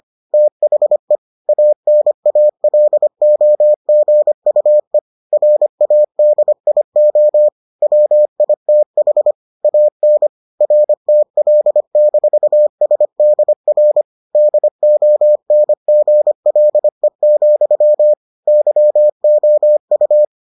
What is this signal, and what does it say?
Morse code, heard as a keyed tone, about 25 wpm, 600 Hz. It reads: THE ANALOGUE RADIO WITH AN RTL-SDR DONGLE, YOU